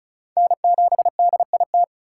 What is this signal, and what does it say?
Morse code, heard as a keyed tone, about 36 wpm, 700 Hz. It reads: N7DIT
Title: USA Call Signs - 5 Characters - 3x Character Spacing - 35wpm